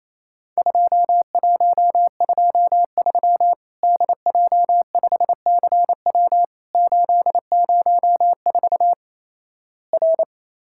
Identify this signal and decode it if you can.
Morse code, heard as a keyed tone, about 28 wpm, 700 Hz. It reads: 2123 DJ5CW 804 R
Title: Worked All Europe DX Contest - QTC Exchange - 28wpm